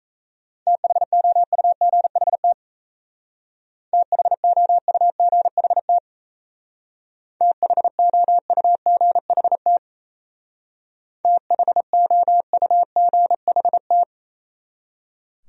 Morse code, heard as a keyed tone, about 37 wpm, 700 Hz. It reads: THOUGHT THOUGHT THOUGHT THOUGHT